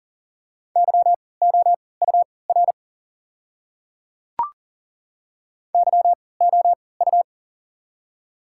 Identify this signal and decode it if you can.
Morse code, heard as a keyed tone, about 40 wpm, spaced as for 23 wpm, 700 Hz. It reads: YOUR YOU
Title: 3x Character Spacing - Top 100 Words - 40wpm